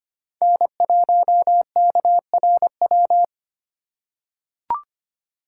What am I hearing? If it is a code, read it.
Morse code, heard as a keyed tone, about 25 wpm, 700 Hz. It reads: N1KRW
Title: USA Call Signs - 5 Characters - 4x Character Spacing - 25wpm